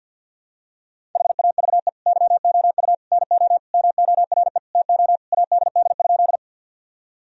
Morse code, heard as a keyed tone, about 52 wpm, 700 Hz. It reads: HAVE YOU NO MORE TO ADD?
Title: Sentences from Top 1000 Words - Review Part 1 - 50wpm